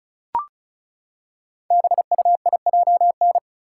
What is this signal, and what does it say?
Morse code, heard as a keyed tone, about 36 wpm, 700 Hz. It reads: BUIJN
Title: Code Groups - 5 Letters - 35wpm